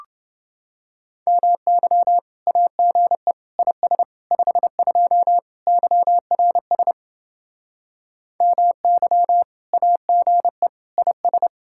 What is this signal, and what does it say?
Morse code, heard as a keyed tone, about 29 wpm, 700 Hz. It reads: MY AGE IS 52 YRS MY AGE IS